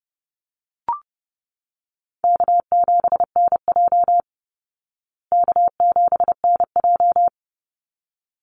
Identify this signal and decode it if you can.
Morse code, heard as a keyed tone, about 30 wpm, 700 Hz. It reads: K7NJ K7NJ
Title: Call Signs Repeated 3x - 30wpm